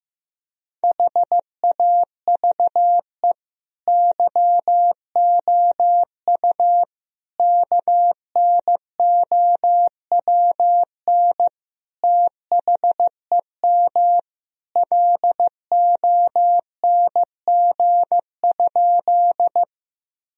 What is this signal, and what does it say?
Morse code, heard as a keyed tone, about 15 wpm, 700 Hz. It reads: HAVE YOU KNOWN THEM LONG?
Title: Sentences from Top 1000 Words - Review Part 3 - 15wpm